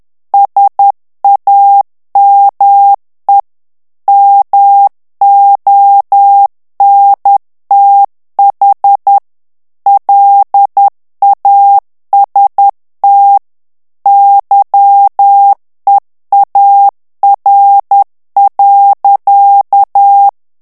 Morse code, heard as a keyed tone, about 10 wpm, 800 Hz. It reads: SAME MONTH LAST YEAR.
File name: morse mp3 / ANGLAIS texte clair 2 vitesse 10 mots minute